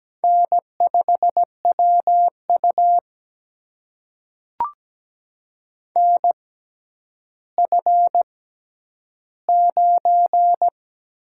Morse code, heard as a keyed tone, about 17 wpm, spaced as for 10 wpm, 700 Hz. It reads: N5WU NF9
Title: USA Call Signs - 4 Characters - 6x Character Spacing - 17wpm